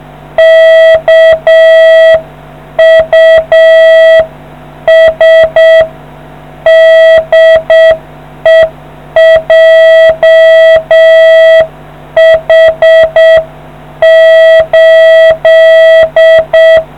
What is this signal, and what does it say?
Morse code, heard as a keyed tone, about 7 wpm, 650 Hz. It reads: KUS DE JH8